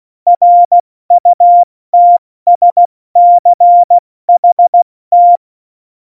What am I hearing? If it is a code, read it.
Morse code, heard as a keyed tone, about 16 wpm, 700 Hz. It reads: RUTSCHT
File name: Morsekurs / Witze / Witz 17. November 2020, 70 F80